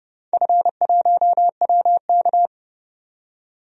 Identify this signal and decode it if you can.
Morse code, heard as a keyed tone, about 30 wpm, 700 Hz. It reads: F1WK
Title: DXCC-100 Call Signs - 4 Characters - 30wpm